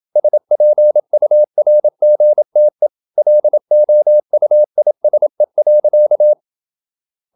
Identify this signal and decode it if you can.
Morse code, heard as a keyed tone, about 28 wpm, 600 Hz. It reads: SPURGTE LOUISE.
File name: Ferie QRQ 27wpm